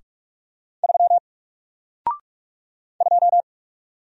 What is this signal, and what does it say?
Morse code, heard as a keyed tone, about 44 wpm, 700 Hz. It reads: 3 2
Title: Single Letter-Number Alt 45wpm